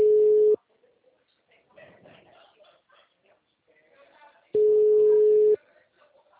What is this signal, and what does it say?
Ringback tone (Europe, 425 Hz), with speech.